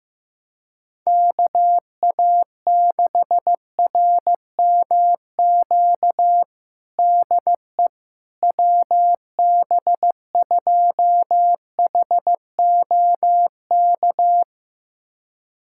Morse code, heard as a keyed tone, about 15 wpm, 700 Hz. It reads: KA6RMQ DE WB2HOK
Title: USA Callsign DE Callsign - 6 Characters - 15wpm